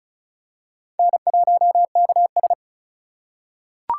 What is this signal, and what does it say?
Morse code, heard as a keyed tone, about 36 wpm, 700 Hz. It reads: N1KS
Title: USA Call Signs - 4 Characters - 3x Character Spacing - 35wpm